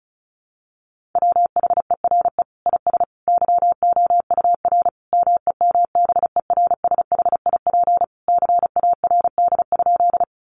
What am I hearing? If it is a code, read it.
Morse code, heard as a keyed tone, about 34 wpm, 700 Hz. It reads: WHERE IS YOUR MEMBERSHIP CARD?
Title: Sentences from Top 1400 Words 35wpm